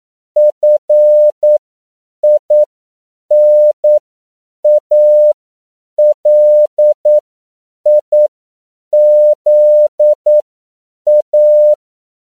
Morse code, heard as a keyed tone, about 9 wpm, spaced as for 7 wpm, 600 Hz. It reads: FINALIZA
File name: CW / 10PPM